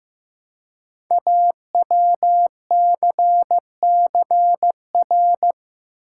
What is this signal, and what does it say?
Morse code, heard as a keyed tone, about 15 wpm, 700 Hz. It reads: AWCCR
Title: Code Groups - 5 Letters - 15wpm